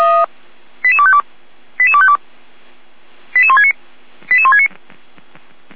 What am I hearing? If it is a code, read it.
Five-tone selective call (ZVEI), heard as quick runs of five tones, tones of about 70 ms. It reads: 88252 88252 88169 88169